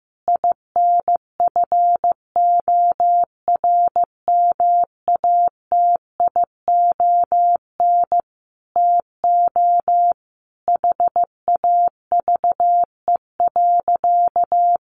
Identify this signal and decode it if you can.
Morse code, heard as a keyed tone, about 15 wpm, 700 Hz. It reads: INFORMATION TO HAVE.
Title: Sentences from Top 1400 Words 15wpm